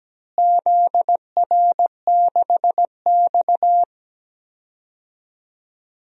Morse code, heard as a keyed tone, about 17 wpm, 700 Hz. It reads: ZR6X
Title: DXCC-100 Call Signs - 4 Characters - Repeated 3x - 17wpm